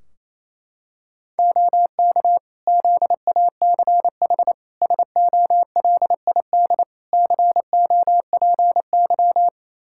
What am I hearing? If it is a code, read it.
Morse code, heard as a keyed tone, about 27 wpm, 700 Hz. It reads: OK ZACH SOLID COPY
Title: K1USN SST QSO Practice 28wpm